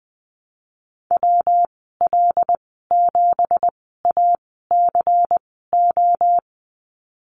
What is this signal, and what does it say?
Morse code, heard as a keyed tone, about 20 wpm, spaced as for 15 wpm, 700 Hz. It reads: WL7ACO
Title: 179 Lesson 29g - Slash - Call Signs 20wpm